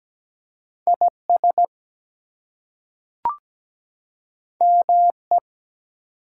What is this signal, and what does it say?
Morse code, heard as a keyed tone, about 17 wpm, 700 Hz. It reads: IS ME
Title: Two Letter Words - 17wpm